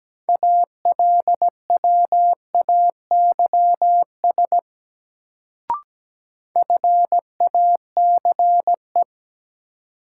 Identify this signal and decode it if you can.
Morse code, heard as a keyed tone, about 17 wpm, 700 Hz. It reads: ALWAYS FACE
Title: Sentences from Top 300 Words - New Word Drill - 17wpm